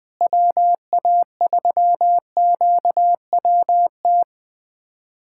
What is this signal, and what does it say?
Morse code, heard as a keyed tone, about 20 wpm, 700 Hz. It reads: WA3QWT
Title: USA Callsign DE Callsign - 6 Characters - 20wpm